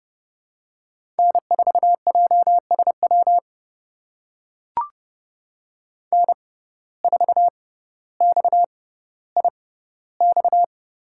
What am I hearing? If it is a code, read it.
Morse code, heard as a keyed tone, about 32 wpm, spaced as for 16 wpm, 700 Hz. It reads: N4JSW N4XIX
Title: USA Call Signs - 5 Characters - 6x Character Spacing - 30wpm